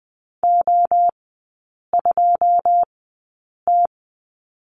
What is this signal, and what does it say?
Morse code, heard as a keyed tone, about 20 wpm, spaced as for 8 wpm, 690 Hz. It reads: O2T
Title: 081 Lesson 14e - Number 2 - Three Characters 20wpm